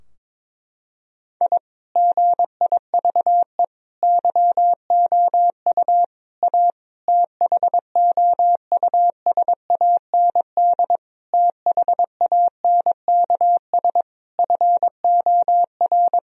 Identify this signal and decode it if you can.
Morse code, heard as a keyed tone, about 22 wpm, 700 Hz. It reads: I GIVE YOU A THOUSAND THANKS FOR
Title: Sentences from Top 1000 Words - Review Part 4 - 22wpm